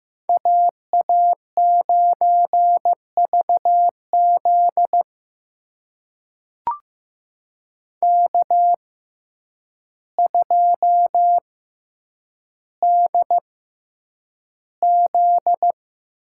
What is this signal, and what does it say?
Morse code, heard as a keyed tone, about 15 wpm, spaced as for 8 wpm, 700 Hz. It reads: AA9VZ K2DZ
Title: USA Call Signs - 6x Character Spacing - 15wpm